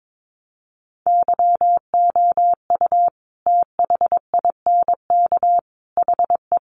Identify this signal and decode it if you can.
Morse code, heard as a keyed tone, about 22 wpm, 700 Hz. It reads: YOU THINK HE
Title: Sentences from Top 800 Words 22wpm